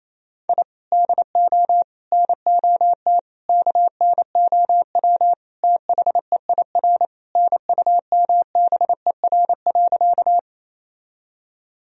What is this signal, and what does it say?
Morse code, heard as a keyed tone, about 28 wpm, 700 Hz. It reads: I DO NOT KNOW THEIR NUMBER.
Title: Sentences from Top 1000 Words - Review Part 1 - 28wpm